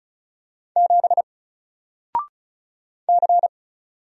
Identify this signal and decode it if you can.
Morse code, heard as a keyed tone, about 36 wpm, 700 Hz. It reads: 7 C
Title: Single Character (Letter:Number:Punctuation:Prosign) 35wpm